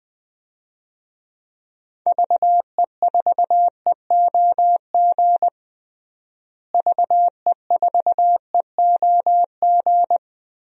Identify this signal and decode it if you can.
Morse code, heard as a keyed tone, about 19 wpm, 700 Hz. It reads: VE4EOG VE4EOG
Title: DXCC-100 Call Signs - 6 Characters - Repeated 3x - 20wpm